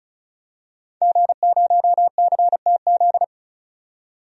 Morse code, heard as a keyed tone, about 34 wpm, 700 Hz. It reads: G0CTZ
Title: DXCC-100 Call Signs - 5 Characters - 35wpm